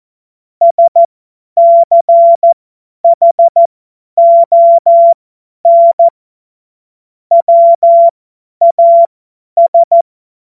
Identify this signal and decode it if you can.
Morse code, heard as a keyed tone, about 14 wpm, spaced as for 10 wpm, 680 Hz. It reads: SCHON WAS
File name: Morsekurs / Witze / Witz 10. Januar 2023 50 F70